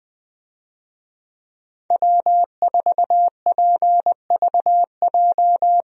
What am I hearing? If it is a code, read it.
Morse code, heard as a keyed tone, about 20 wpm, 700 Hz. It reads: W4PVJ
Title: USA Call Signs - 5 Characters - Repeated 2x - 20wpm